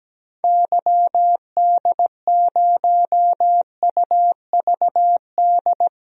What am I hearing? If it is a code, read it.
Morse code, heard as a keyed tone, about 17 wpm, 700 Hz. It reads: YD0UVD
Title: DXCC-100 Call Signs - 6 Characters - 17wpm